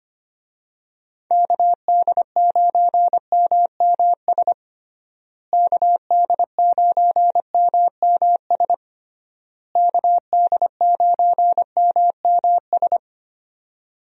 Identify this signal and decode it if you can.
Morse code, heard as a keyed tone, about 25 wpm, 700 Hz. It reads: KD9MMS KD9MMS KD9MMS
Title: USA Call Signs - 6 Characters - Repeated 3x - 25wpm